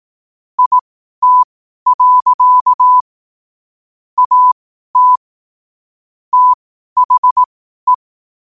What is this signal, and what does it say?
Morse code, heard as a keyed tone, about 18 wpm, spaced as for 12 wpm, 1000 Hz. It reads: IT. AT THE
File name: qotdfiles / 12wpm